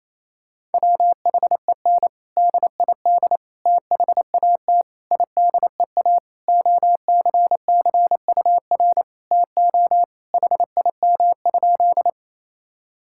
Morse code, heard as a keyed tone, about 27 wpm, 700 Hz. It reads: WHEN DID THAT IDEA OCCUR TO HIM?
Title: Sentences from Top 1000 Words - Review Part 6 - 28wpm